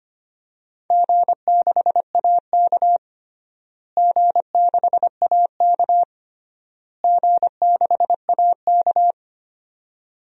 Morse code, heard as a keyed tone, about 25 wpm, 700 Hz. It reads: G6AK G6AK G6AK
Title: DXCC-100 Call Signs - 4 Characters - Repeated 3x - 25wpm